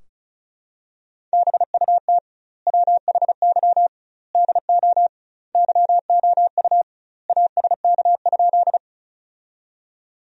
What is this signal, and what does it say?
Morse code, heard as a keyed tone, about 36 wpm, 700 Hz. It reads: BUT WHY DO YOU ASK?
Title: 2x Word-Spacing - Sentences from Top 200 Words - 35wpm